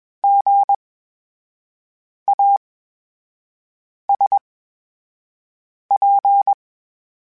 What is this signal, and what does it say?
Morse code, heard as a keyed tone, about 21 wpm, spaced as for 5 wpm, 800 Hz. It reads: GASP
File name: Special INGs-mp3s / Special INGs-mp3s / VerySlowToFaster Ing @20